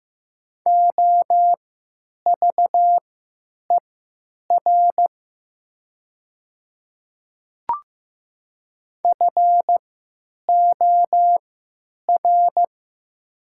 Morse code, heard as a keyed tone, about 15 wpm, spaced as for 8 wpm, 700 Hz. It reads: OVER FOR
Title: 3x Character Spacing - Top 100 Words - 15wpm